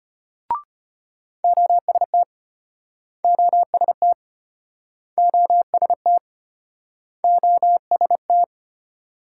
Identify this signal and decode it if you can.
Morse code, heard as a keyed tone, about 35 wpm, 700 Hz. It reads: OST OST OST OST